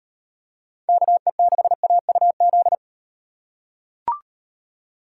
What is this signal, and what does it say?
Morse code, heard as a keyed tone, about 36 wpm, 700 Hz. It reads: KE6AUZ